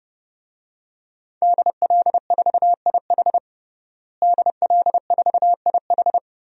Morse code, heard as a keyed tone, about 30 wpm, 700 Hz. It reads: DL4IH DL4IH
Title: DXCC-100 Call Signs - 5 Characters - Repeated 3x - 30wpm